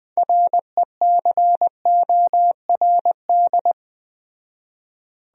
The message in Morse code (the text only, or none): RECORD